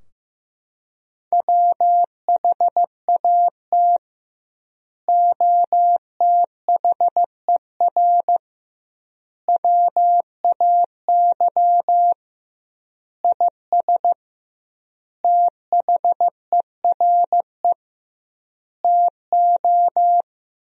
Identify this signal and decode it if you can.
Morse code, heard as a keyed tone, about 16 wpm, 700 Hz. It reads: WHAT OTHER WAY IS THERE TO
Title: 2x Word-Spacing - Sentences from Top 100 Words - 15wpm